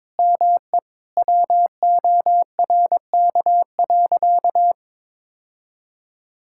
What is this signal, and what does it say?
Morse code, heard as a keyed tone, about 22 wpm, 700 Hz. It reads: ME WORK.
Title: Sentences from Top 1600 Words 22wpm